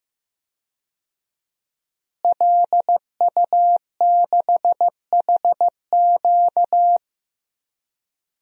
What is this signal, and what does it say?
Morse code, heard as a keyed tone, about 15 wpm, 700 Hz. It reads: LU6HQ
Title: DXCC-100 Call Signs - 5 Characters - Repeated 2x - 15wpm